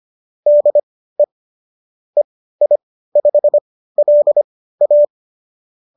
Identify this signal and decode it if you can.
Morse code, heard as a keyed tone, about 25 wpm, spaced as for 15 wpm, 590 Hz. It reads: DE EI5LA